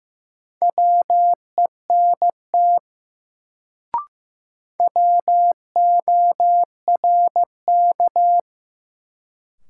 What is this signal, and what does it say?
Morse code, heard as a keyed tone, about 15 wpm, 700 Hz. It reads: WENT WORK
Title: Sentences from Top 200 Words - New Word Drill - 15wpm